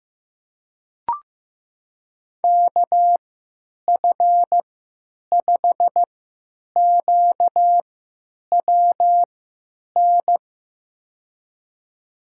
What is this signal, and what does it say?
Morse code, heard as a keyed tone, about 15 wpm, spaced as for 9 wpm, 700 Hz. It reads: KF5QWN
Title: USA Call Signs - 6 Characters - 3x Character Spacing - 15wpm